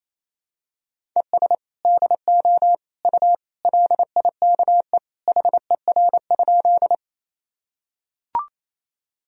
Morse code, heard as a keyed tone, about 28 wpm, 700 Hz. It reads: ES DO U LIKE HER?
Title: Sentences from Top 100 Words and Abbreviations 28wpm